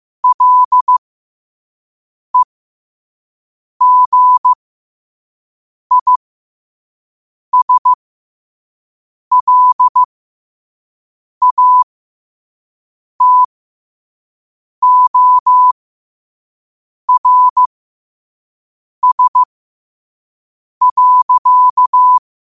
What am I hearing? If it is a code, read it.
Morse code, heard as a keyed tone, about 15 wpm, spaced as for 5 wpm, 1000 Hz. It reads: LEGISLATORS.
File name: qotdfiles / 5wpm